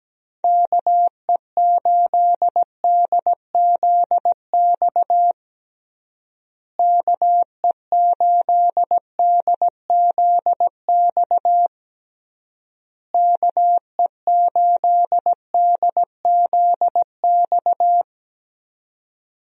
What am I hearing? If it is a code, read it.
Morse code, heard as a keyed tone, about 17 wpm, 700 Hz. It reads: KE8DZX KE8DZX KE8DZX
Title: USA Call Signs - 6 Characters - Repeated 3x - 17wpm